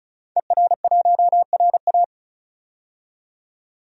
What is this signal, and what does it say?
Morse code, heard as a keyed tone, about 34 wpm, 700 Hz. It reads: ER1RA
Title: DXCC-100 Call Signs - 5 Characters - 35wpm